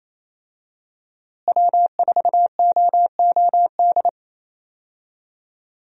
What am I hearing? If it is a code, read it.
Morse code, heard as a keyed tone, about 27 wpm, 700 Hz. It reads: W4OOD
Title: USA Call Signs - 5 Characters - Repeated 2x - 28wpm